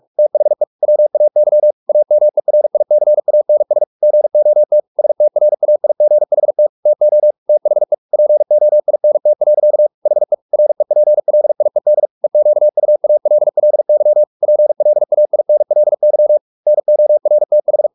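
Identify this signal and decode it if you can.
Morse code, heard as a keyed tone, about 44 wpm, 600 Hz. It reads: THE WAY AMERICANS GOT STRAIGHT TO THE POINT. HE REPLIED EQUALLY PLAINLY NORTH
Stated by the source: Morse practice at 45 wpm, the novel Never by Ken Follett in English, from CW med Gnister.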